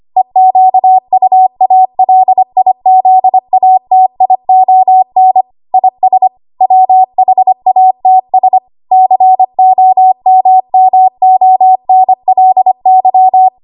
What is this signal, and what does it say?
Morse code, heard as a keyed tone, about 25 wpm, 750 Hz. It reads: EQUALIZATION IS WHATS COMMONLY
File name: Morse / 25WPM